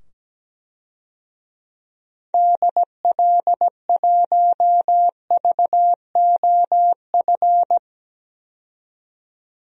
Morse code, heard as a keyed tone, about 17 wpm, 700 Hz. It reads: DL1VOF
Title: DXCC-100 Call Signs - 6 Characters - 17wpm